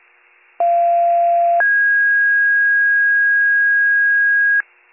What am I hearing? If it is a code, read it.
Two-tone page, heard as a one-second tone, then a three-second tone, about 690 then 1650 Hz. It is heard from 0.6 to 4.6 s.